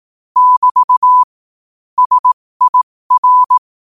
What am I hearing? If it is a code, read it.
Morse code, heard as a keyed tone, about 18 wpm, spaced as for 15 wpm, 1000 Hz. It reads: <BT> SIR
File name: qotdfiles / 15wpm